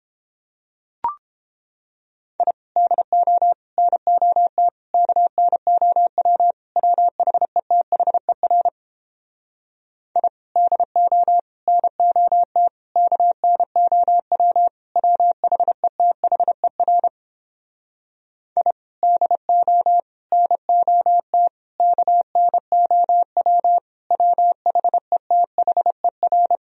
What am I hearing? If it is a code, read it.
Morse code, heard as a keyed tone, about 31 wpm, 700 Hz. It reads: I DO NOT KNOW WHETHER I DO NOT KNOW WHETHER I DO NOT KNOW WHETHER